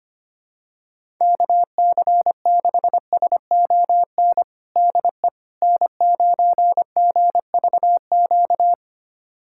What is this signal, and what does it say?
Morse code, heard as a keyed tone, about 25 wpm, 700 Hz. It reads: KC6SON DE N9GVQ
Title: USA Callsign DE Callsign 25wpm